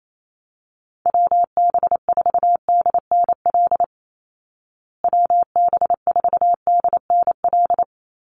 Morse code, heard as a keyed tone, about 28 wpm, 700 Hz. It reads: WB4DNL WB4DNL
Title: Call Signs Repeated 2x - 28wpm